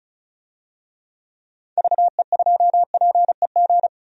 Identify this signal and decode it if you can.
Morse code, heard as a keyed tone, about 36 wpm, 700 Hz. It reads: VE2PEG
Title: DXCC-100 Call Signs - 6 Characters - 35wpm